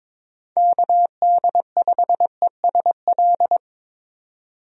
Morse code, heard as a keyed tone, about 22 wpm, 700 Hz. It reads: KD5ESL
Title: USA Call Signs - 6 Characters - 22wpm